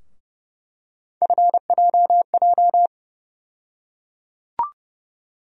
Morse code, heard as a keyed tone, about 30 wpm, 700 Hz. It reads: FJJ
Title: USA Call Sign Postfixes - 3 Letters 30wpm